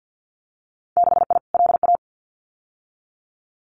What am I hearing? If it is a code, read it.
Morse code, heard as a keyed tone, about 50 wpm, 700 Hz. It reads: 6I LA
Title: Winter Field Day Exchanges Encore 50wpm